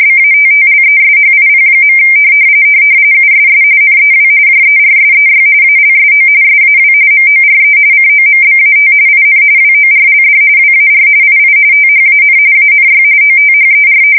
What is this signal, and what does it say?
RTTY, about 45.45 baud. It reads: BATTLESHIP. USS NEW JERSEY ALSO SAW MORE COMBAT IN WORLD WAR II THAN ANY OTHER
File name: fsk170 / battleshipnj